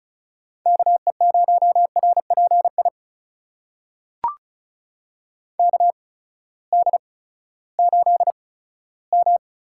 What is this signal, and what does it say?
Morse code, heard as a keyed tone, about 36 wpm, spaced as for 18 wpm, 700 Hz. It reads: KE0RPI KD8M
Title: USA Call Signs - 8x Character Spacing - 35wpm